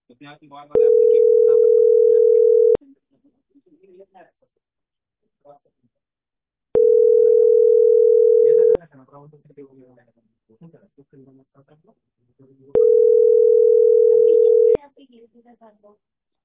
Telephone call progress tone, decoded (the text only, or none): ringback tone